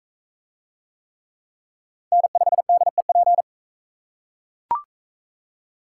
Morse code, heard as a keyed tone, about 44 wpm, 700 Hz. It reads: N5DEP